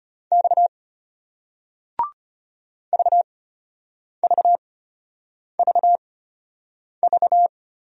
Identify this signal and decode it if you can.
Morse code, heard as a keyed tone, about 34 wpm, spaced as for 8 wpm, 700 Hz. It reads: X VVVV